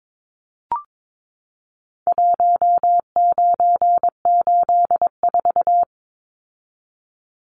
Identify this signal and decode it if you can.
Morse code, heard as a keyed tone, about 22 wpm, 700 Hz. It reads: 1984
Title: Numbers - Contest Style - 4-Digit 22wpm